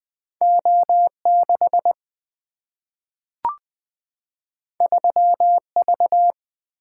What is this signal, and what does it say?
Morse code, heard as a keyed tone, about 20 wpm, 700 Hz. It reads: O 6 3 V
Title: Single-Letter Single-Number 20wpm